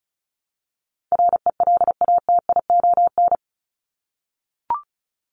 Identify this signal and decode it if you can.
Morse code, heard as a keyed tone, about 34 wpm, 700 Hz. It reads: RELATION
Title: ES:S Suffix - Top 100 Words with Base Words 35wpm